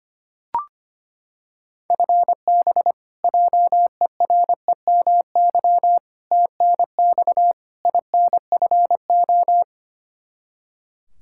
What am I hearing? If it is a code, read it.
Morse code, heard as a keyed tone, about 25 wpm, 700 Hz. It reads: FB JEREMY TNX INFO
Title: Practice QSO - Exchange 3 - More Info - 25wpm